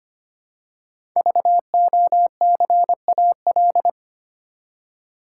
Morse code, heard as a keyed tone, about 25 wpm, 700 Hz. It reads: VOCAL
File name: 5 Letter Words - First 3 Letters Given - 25wpm